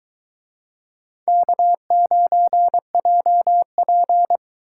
Morse code, heard as a keyed tone, about 22 wpm, 700 Hz. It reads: K9JP